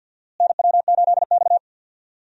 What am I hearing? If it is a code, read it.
Morse code, heard as a keyed tone, about 51 wpm, 700 Hz. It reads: NW8X
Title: USA Call Signs - 4 Characters - Repeated 3x - 50wpm